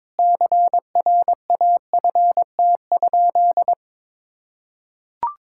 Morse code, heard as a keyed tone, about 22 wpm, 700 Hz. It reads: CRAFT?
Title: Sentences from Top 1000 Words - Review Part 6 - 22wpm